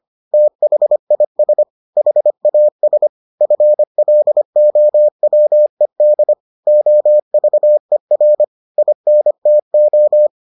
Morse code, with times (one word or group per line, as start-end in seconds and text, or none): 0.3-1.6 THIS
2.0-3.1 HAS
3.4-6.3 FLOWED
6.7-8.4 OVER
8.8-10.3 INTO